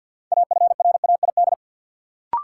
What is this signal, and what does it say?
Morse code, heard as a keyed tone, about 53 wpm, 700 Hz. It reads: AFRAID
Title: Sentences from Top 1500 Words - New Word Drill - 50wpm